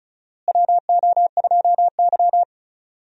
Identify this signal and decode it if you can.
Morse code, heard as a keyed tone, about 34 wpm, 700 Hz. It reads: WO2Y
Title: USA Call Signs - 4 Characters - Repeated 3x - 35wpm